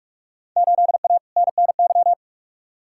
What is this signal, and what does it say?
Morse code, heard as a keyed tone, about 45 wpm, 700 Hz. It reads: 8A NNY
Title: Field Day Exchanges Encore - 45wpm